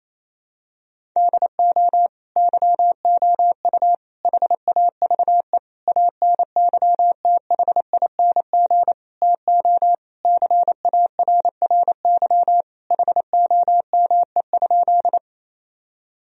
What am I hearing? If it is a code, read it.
Morse code, heard as a keyed tone, about 28 wpm, 700 Hz. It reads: DO YOU HAVE ANYTHING TO CARRY HOME?
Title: Sentences from Top 1000 Words - Review Part 1 - 28wpm